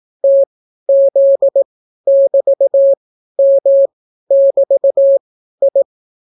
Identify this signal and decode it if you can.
Morse code, heard as a keyed tone, about 18 wpm, spaced as for 12 wpm, 550 Hz. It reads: TZ<BT>M<BT>I